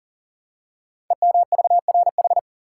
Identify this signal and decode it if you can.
Morse code, heard as a keyed tone, about 40 wpm, 700 Hz. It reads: EMVRH
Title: Code Groups - 5 Letters - 40wpm